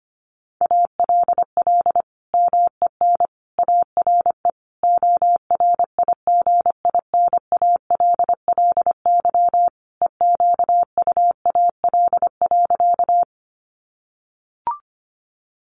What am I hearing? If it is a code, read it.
Morse code, heard as a keyed tone, about 25 wpm, 700 Hz. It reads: ALL MEN ARE ORIGINALLY EQUAL.
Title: Sentences from Top 1500 Words 25wpm